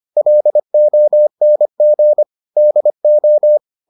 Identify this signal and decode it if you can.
Morse code, heard as a keyed tone, about 25 wpm, 600 Hz. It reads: LONG DO